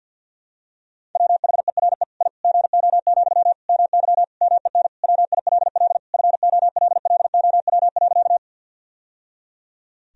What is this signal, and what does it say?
Morse code, heard as a keyed tone, about 52 wpm, 700 Hz. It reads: WHERE I GO, MY MEN WILL FOLLOW.
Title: Sentences from Top 1000 Words - Review Part 1 - 50wpm